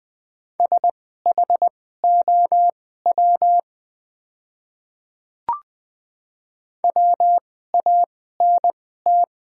Morse code, heard as a keyed tone, about 20 wpm, spaced as for 14 wpm, 700 Hz. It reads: SHOW WANT
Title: 2x Character Spacing - Top 200 Words - 20wpm